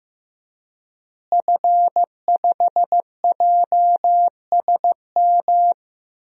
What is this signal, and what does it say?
Morse code, heard as a keyed tone, about 15 wpm, 700 Hz. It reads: F5JSM
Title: DXCC-100 Call Signs - 5 Characters - Repeated 3x - 15wpm